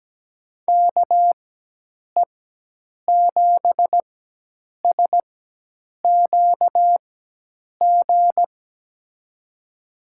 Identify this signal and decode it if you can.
Morse code, heard as a keyed tone, about 17 wpm, spaced as for 8 wpm, 700 Hz. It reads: KE7SQG
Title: USA Call Signs - 6 Characters - 4x Character Spacing - 17wpm